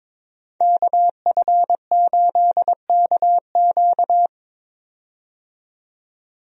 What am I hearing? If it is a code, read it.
Morse code, heard as a keyed tone, about 22 wpm, 700 Hz. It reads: KF8KQ
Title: USA Call Signs - 5 Characters - 22wpm